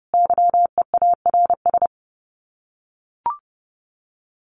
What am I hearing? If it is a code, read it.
Morse code, heard as a keyed tone, about 28 wpm, 700 Hz. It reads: YEARS
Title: Sets of 5 Words 30wpm